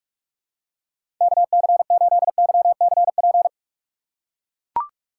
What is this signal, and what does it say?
Morse code, heard as a keyed tone, about 44 wpm, 700 Hz. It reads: KC8YCP
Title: USA Call Signs - 6 Characters - 6x Character Spacing - 45wpm